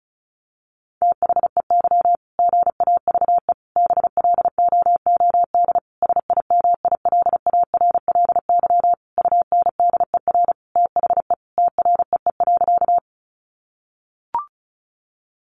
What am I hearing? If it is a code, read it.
Morse code, heard as a keyed tone, about 34 wpm, 700 Hz. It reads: THEY GAVE BLOOD SIMILARLY UNDER THE TREE.
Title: Sentences from Top 1400 Words 35wpm